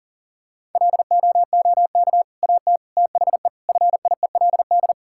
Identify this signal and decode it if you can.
Morse code, heard as a keyed tone, about 40 wpm, 700 Hz. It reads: LOOK AT THE FIELD
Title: Sentences from Top 1000 Words - Review Part 5 - 40wpm